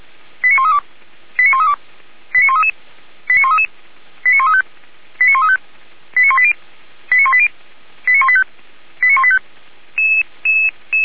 Five-tone selective call (ZVEI), heard as quick runs of five tones, tones of about 70 ms. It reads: 89132 89132 89133 89133 89135 89135 89180 89180 89185 89185